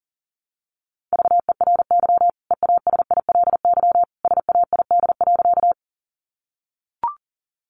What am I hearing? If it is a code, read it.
Morse code, heard as a keyed tone, about 40 wpm, 700 Hz. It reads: VERY EASILY SAID.
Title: Sentences from Top 900 Words 40wpm